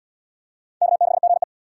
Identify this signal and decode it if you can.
Morse code, heard as a keyed tone, about 69 wpm, 700 Hz. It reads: K6GE